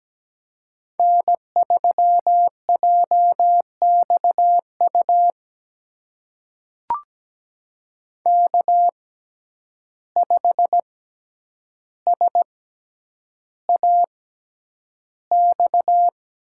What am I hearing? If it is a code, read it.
Morse code, heard as a keyed tone, about 17 wpm, spaced as for 9 wpm, 700 Hz. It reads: N3JXU K5SAX
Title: USA Call Signs - 5 Characters - 6x Character Spacing - 17wpm